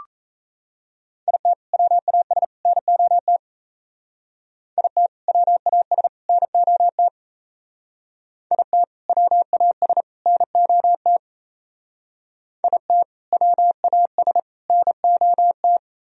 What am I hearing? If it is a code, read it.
Morse code, heard as a keyed tone, about 36 wpm, 700 Hz. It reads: IT WAS NOT IT WAS NOT IT WAS NOT IT WAS NOT